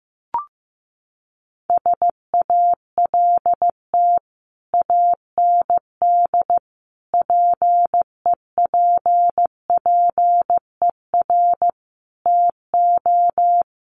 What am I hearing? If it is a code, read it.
Morse code, heard as a keyed tone, about 15 wpm, 700 Hz. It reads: SALT AND PEPPER TO
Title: Sets of 5 Words 15wpm